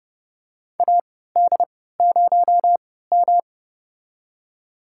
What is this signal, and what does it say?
Morse code, heard as a keyed tone, about 31 wpm, spaced as for 17 wpm, 700 Hz. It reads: AD0M
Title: USA Call Signs - 3x Character Spacing - 30wpm